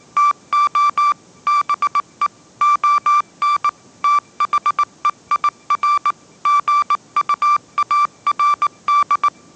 Morse code, heard as a keyed tone, about 19 wpm, 1170 Hz. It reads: TO BE ON THEIR GUARD
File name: 20wpm